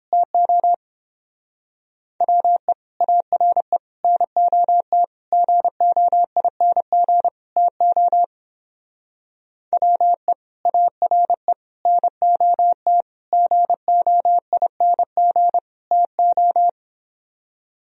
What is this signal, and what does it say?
Morse code, heard as a keyed tone, about 29 wpm, 700 Hz. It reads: TO WE ARE NOT GOING TO WE ARE NOT GOING TO